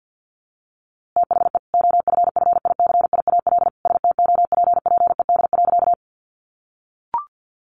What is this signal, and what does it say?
Morse code, heard as a keyed tone, about 48 wpm, 700 Hz. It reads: THE OFFICIAL STOPPED.
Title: A Sound Of Thunder 50wpm